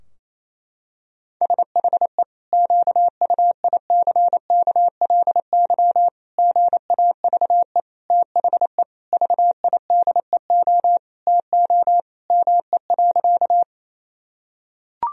Morse code, heard as a keyed tone, about 28 wpm, 700 Hz. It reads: SHE QUICKLY GAVE THE VIDEO TO ME.
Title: Sentences from Top 1000 Words - Review Part 4 - 28wpm